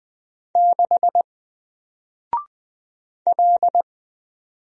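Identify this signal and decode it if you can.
Morse code, heard as a keyed tone, about 20 wpm, 700 Hz. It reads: 6 L
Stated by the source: Single letter and number Morse drill at 20 wpm, from N4LCD.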